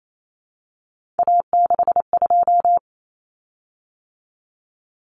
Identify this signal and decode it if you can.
Morse code, heard as a keyed tone, about 28 wpm, 700 Hz. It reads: A62
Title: Common Cut Numbers - 3-Digit 28wpm